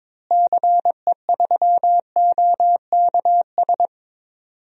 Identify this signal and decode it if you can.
Morse code, heard as a keyed tone, about 22 wpm, 700 Hz. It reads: CE3OKS
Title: DXCC-100 Call Signs - 6 Characters - Repeated 2x - 22wpm